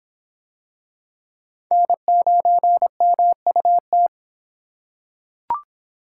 Morse code, heard as a keyed tone, about 23 wpm, 700 Hz. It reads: N9MUT